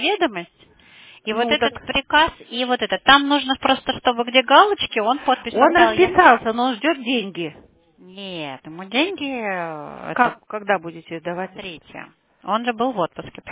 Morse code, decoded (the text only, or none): none